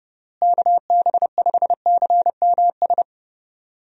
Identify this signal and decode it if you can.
Morse code, heard as a keyed tone, about 30 wpm, 700 Hz. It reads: KB5CMS
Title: USA Call Signs - 6 Characters - Repeated 3x - 30wpm